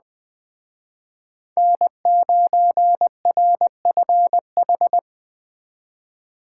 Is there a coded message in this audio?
Yes: Morse code from 1.6 to 5.0 s, heard as a keyed tone, about 20 wpm, 700 Hz. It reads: N9RFH